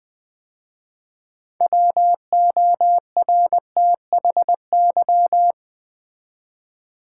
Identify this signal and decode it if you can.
Morse code, heard as a keyed tone, about 20 wpm, 700 Hz. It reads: WORTHY
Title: 6 Letter Words - First 3 Letters Given - 20wpm